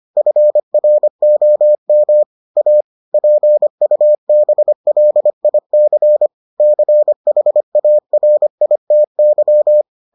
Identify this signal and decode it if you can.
Morse code, heard as a keyed tone, about 25 wpm, 600 Hz. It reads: FROM A PUBLIC CHARITY